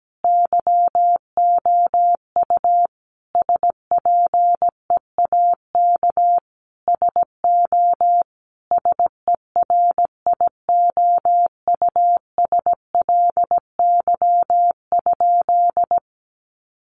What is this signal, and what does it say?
Morse code, heard as a keyed tone, about 17 wpm, 700 Hz. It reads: YOU SPEAK SO SERIOUSLY?
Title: Sentences from Top 1200 Words 17wpm